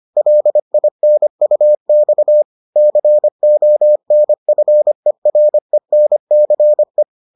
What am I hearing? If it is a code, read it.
Morse code, heard as a keyed tone, about 25 wpm, 600 Hz. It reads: LINUX CONFERENCE